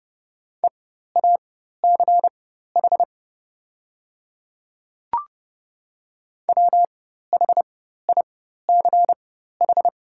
Morse code, heard as a keyed tone, about 30 wpm, spaced as for 14 wpm, 700 Hz. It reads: EACH WHICH
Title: 4x Character Spacing - Top 100 Words - 30wpm